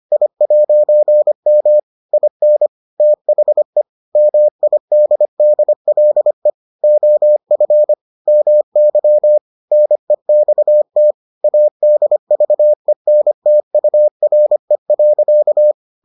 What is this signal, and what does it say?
Morse code, heard as a keyed tone, about 25 wpm, 600 Hz. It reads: I'M IN THE MIDDLE OF MY NEXT ADVENTURE.